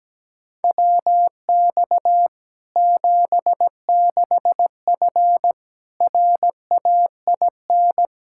Morse code, heard as a keyed tone, about 17 wpm, 700 Hz. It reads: WX 76F RAIN
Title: Practice QSO - Exchange 2 - Info - 17wpm